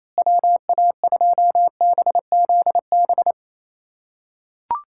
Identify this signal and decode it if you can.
Morse code, heard as a keyed tone, about 28 wpm, 700 Hz. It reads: WA2BZB
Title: USA Callsign DE Callsign - 6 Characters - 28wpm